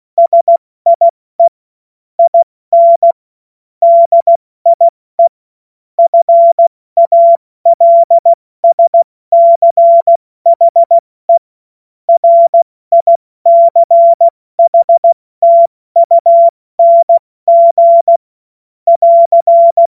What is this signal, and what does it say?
Morse code, heard as a keyed tone, about 16 wpm, 680 Hz. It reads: SIE IN DIE FALSCHE RICHTUNG <AR>
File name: Morsekurs / Witze / Witz 25. Januar 2022, 70 F80